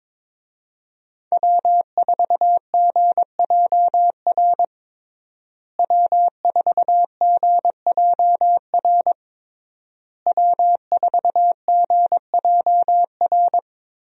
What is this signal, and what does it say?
Morse code, heard as a keyed tone, about 22 wpm, 700 Hz. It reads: W4GJR W4GJR W4GJR
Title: USA Call Signs - 5 Characters - Repeated 3x - 22wpm